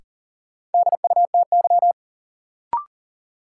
Morse code, heard as a keyed tone, about 37 wpm, 700 Hz. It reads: DUTY
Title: Sentences from Top 1500 Words - New Word Drill Alt - 40wpm